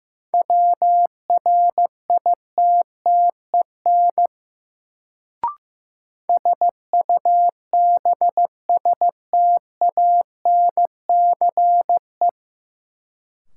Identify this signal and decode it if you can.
Morse code, heard as a keyed tone, about 15 wpm, 700 Hz. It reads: WRITTEN SUBSTANCE
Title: Sentences from Top 1500 Words - New Word Drill Alt - 15wpm